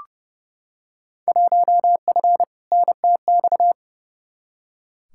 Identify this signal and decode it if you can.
Morse code, heard as a keyed tone, about 29 wpm, 700 Hz. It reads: 1F NTX
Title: Field Day Exchanges Encore - 30wpm